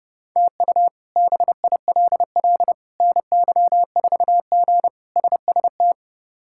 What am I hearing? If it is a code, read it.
Morse code, heard as a keyed tone, about 30 wpm, 700 Hz. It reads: TU BILL NY4G SST
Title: K1USN SST QSO Practice - 30wpm